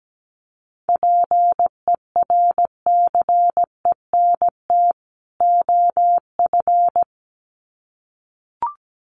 Morse code, heard as a keyed tone, about 17 wpm, 700 Hz. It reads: PERCENT OF
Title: Sets of 2 Words 17wpm